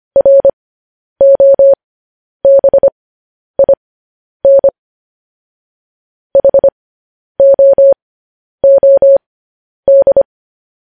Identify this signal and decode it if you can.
Morse code, heard as a keyed tone, about 25 wpm, spaced as for 10 wpm, 550 Hz. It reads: ROBIN HOOD
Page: news-headlines-in-morse-code-at-10-wpm